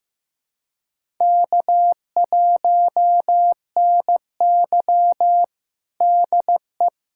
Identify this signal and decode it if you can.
Morse code, heard as a keyed tone, about 15 wpm, 700 Hz. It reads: K1NY DE
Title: USA Callsign DE Callsign - 4 Characters - 15wpm